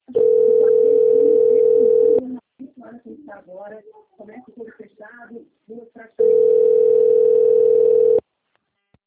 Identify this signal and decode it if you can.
Ringback tone (North America), 440 + 480 Hz, with speech.
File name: out- - - - - .17298